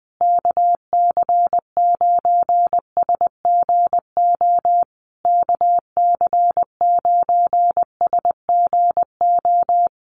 Morse code, heard as a keyed tone, about 20 wpm, 700 Hz. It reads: KC9SGO KC9SGO
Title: USA Call Signs Repeated 3x 20wpm